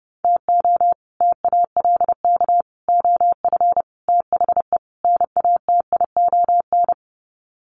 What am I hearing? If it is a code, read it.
Morse code, heard as a keyed tone, about 30 wpm, 700 Hz. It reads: TO TALK OF THE NATION
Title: Sets of 5 Words 30wpm